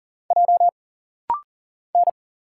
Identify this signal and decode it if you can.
Morse code, heard as a keyed tone, about 40 wpm, 700 Hz. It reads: J N
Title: Single Letters - Rapid-Fire 40wpm